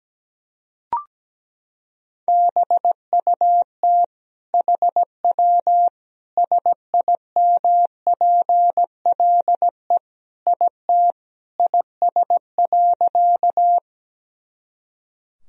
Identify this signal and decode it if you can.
Morse code, heard as a keyed tone, about 17 wpm, 700 Hz. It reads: BUT HW SIMPLE IT IS.
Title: Sentences from Top 400 Words and Abbreviations 17wpm